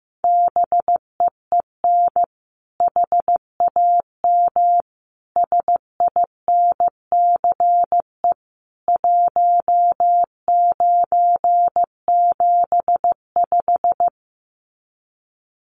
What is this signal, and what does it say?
Morse code, heard as a keyed tone, about 15 wpm, 700 Hz. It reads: BEEN HAM SINCE 1975
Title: Common QSO Phrases 15wpm